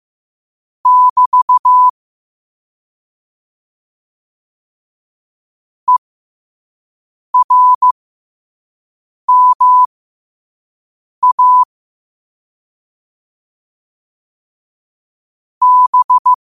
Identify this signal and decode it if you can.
Morse code, heard as a keyed tone, about 15 wpm, spaced as for 5 wpm, 1000 Hz. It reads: <BT> ERMA B
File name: qotdfiles / 5wpm